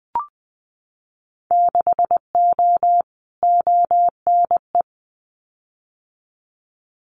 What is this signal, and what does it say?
Morse code, heard as a keyed tone, about 19 wpm, 700 Hz. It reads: 6O ONE
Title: Winter Field Day Exchanges Encore 20wpm